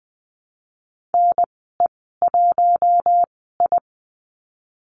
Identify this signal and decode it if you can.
Morse code, heard as a keyed tone, about 20 wpm, spaced as for 15 wpm, 700 Hz. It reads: NE1I
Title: 055 Lesson 10g - Letter r - Call Signs 20wpm